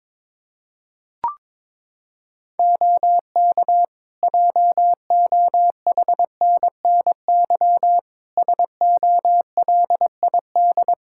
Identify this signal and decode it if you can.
Morse code, heard as a keyed tone, about 22 wpm, 700 Hz. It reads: OK JOHNNY SOLID
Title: Practice QSO - Exchange 3 - More Info - 22wpm